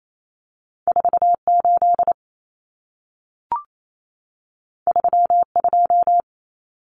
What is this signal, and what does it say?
Morse code, heard as a keyed tone, about 28 wpm, 700 Hz. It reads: 48 32
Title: Common Cut Numbers - 2-Digit 28wpm